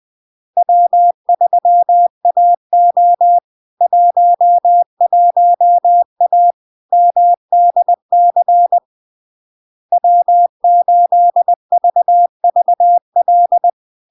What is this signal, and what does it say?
Morse code, heard as a keyed tone, about 20 wpm, 700 Hz. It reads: W3AO 11A MDC W8VVL
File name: LARK-FD-2022-Log-in-CW-20wpm-700-Hz